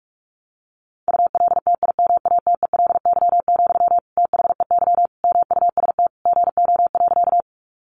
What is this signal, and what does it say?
Morse code, heard as a keyed tone, about 44 wpm, 700 Hz. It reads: ULTIMATELY, THEY MUST GO.
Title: Sentences from Top 1200 Words 45wpm